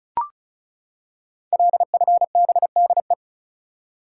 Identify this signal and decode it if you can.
Morse code, heard as a keyed tone, about 36 wpm, 700 Hz. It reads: LFBDE
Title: Code Groups - 5 Letters - 35wpm